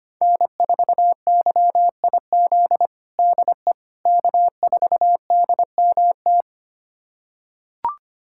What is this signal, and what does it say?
Morse code, heard as a keyed tone, about 25 wpm, 700 Hz. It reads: N4YIZ DE K4DMT
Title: USA Callsign DE Callsign - 5 Characters - 25wpm